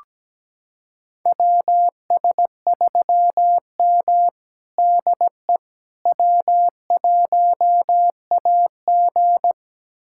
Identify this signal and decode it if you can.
Morse code, heard as a keyed tone, about 17 wpm, 700 Hz. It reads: WS3M DE W1AG
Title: USA Callsign DE Callsign - 4 Characters - 17wpm